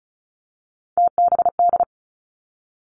Morse code, heard as a keyed tone, about 36 wpm, 700 Hz. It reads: TBD
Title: Three Challenging Letter-Number Combinations 35wpm